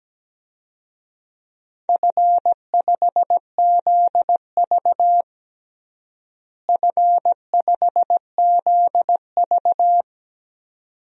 Morse code, heard as a keyed tone, about 17 wpm, 700 Hz. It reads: F5ZV F5ZV
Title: DXCC-100 Call Signs - 4 Characters - Repeated 2x - 17wpm